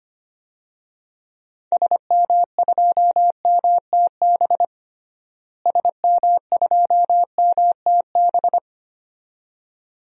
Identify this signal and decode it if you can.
Morse code, heard as a keyed tone, about 25 wpm, 700 Hz. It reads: SM2MTB SM2MTB
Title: DXCC-100 Call Signs - 6 Characters - Repeated 2x - 25wpm